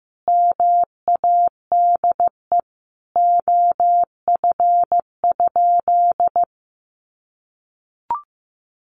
Morse code, heard as a keyed tone, about 15 wpm, 700 Hz. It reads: MADE OF?
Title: Sentences from Top 200 Words 15wpm